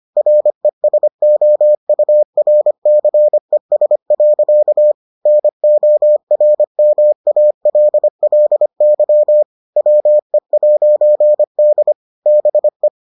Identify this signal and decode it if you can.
Morse code, heard as a keyed tone, about 25 wpm, 600 Hz. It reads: RESOURCES. NORMALLY WE'D BE